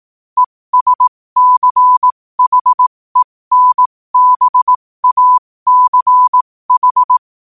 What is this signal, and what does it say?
Morse code, heard as a keyed tone, about 18 wpm, spaced as for 15 wpm, 1000 Hz. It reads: ESCHENBACH
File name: qotdfiles / 15wpm